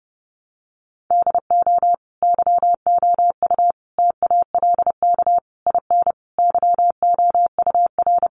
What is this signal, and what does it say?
Morse code, heard as a keyed tone, about 30 wpm, 700 Hz. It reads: DO YOU TALK IN YOUR
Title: Sentences from Top 500 Words 30wpm